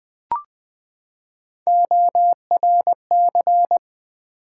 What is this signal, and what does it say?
Morse code, heard as a keyed tone, about 20 wpm, 700 Hz. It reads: ORC
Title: Build A Word - 3 Letters At A Time - 9-Letter Words - 20wpm